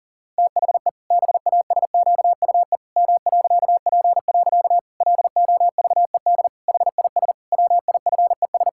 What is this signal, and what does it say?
Morse code, heard as a keyed tone, about 37 wpm, 700 Hz. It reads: THE BASQUE M.P. LOVED HIS WIFES
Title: Pangram Challenge v2 - 40wpm